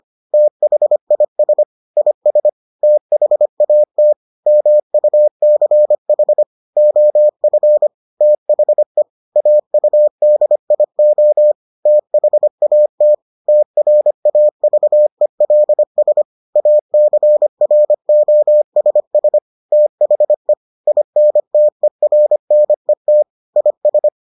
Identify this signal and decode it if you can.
Morse code, heard as a keyed tone, about 25 wpm, 600 Hz. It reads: THIS IS THAT MUCH OF THE AUDIO THAT TRAVELS ACROSS THE INTERNET IS